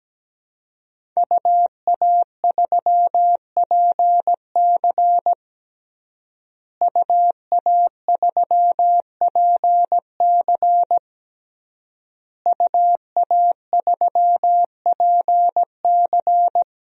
Morse code, heard as a keyed tone, about 17 wpm, 700 Hz. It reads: UA3PC UA3PC UA3PC
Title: DXCC-100 Call Signs - 5 Characters - Repeated 3x - 17wpm